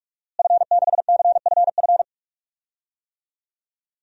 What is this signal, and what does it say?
Morse code, heard as a keyed tone, about 43 wpm, 700 Hz. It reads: F6CFF
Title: DXCC-100 Call Signs - 5 Characters - Repeated 3x - 45wpm